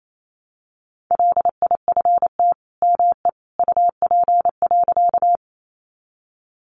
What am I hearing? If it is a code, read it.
Morse code, heard as a keyed tone, about 28 wpm, 700 Hz. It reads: LIFT ME UP.
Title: Sentences from Top 1000 Words 28wpm